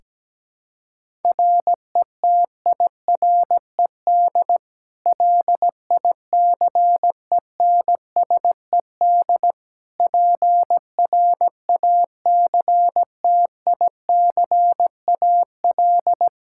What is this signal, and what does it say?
Morse code, heard as a keyed tone, about 17 wpm, 700 Hz. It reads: RETIRED LICENSED PRACTICAL
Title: Practice QSO - Exchange 3 - More Info - 17wpm